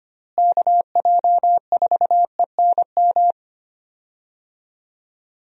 Morse code, heard as a keyed tone, about 25 wpm, 700 Hz. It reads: KJ4ENM
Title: USA Call Signs - 6 Characters - Repeated 2x - 25wpm